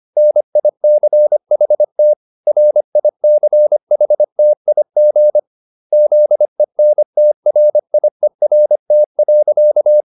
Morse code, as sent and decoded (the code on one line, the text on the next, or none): -. .. -.-. .... - / .-. .. -.-. .... - .. --. / --.. -. - .-. .. .-. - .-.-.-
NICHT RICHTIG ZENTRIERT.